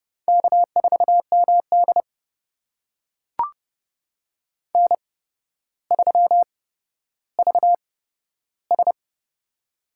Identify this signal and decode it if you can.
Morse code, heard as a keyed tone, about 32 wpm, spaced as for 13 wpm, 700 Hz. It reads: K4MD N3VS
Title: USA Call Signs - 4 Characters - 8x Character Spacing - 30wpm